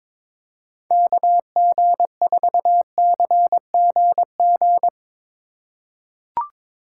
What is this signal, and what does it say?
Morse code, heard as a keyed tone, about 22 wpm, 700 Hz. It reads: KG4CGG
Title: USA Call Signs - 3x Character Spacing - 22wpm